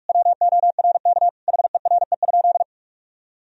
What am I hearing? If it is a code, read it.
Morse code, heard as a keyed tone, about 43 wpm, 700 Hz. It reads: WORK HERE?
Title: Sentences from Top 1000 Words - Review Part 4 - 45wpm